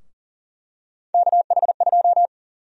Morse code, heard as a keyed tone, about 40 wpm, 700 Hz. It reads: KH2
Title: USA Call Sign Prefixes - Repeated 2x - 40wpm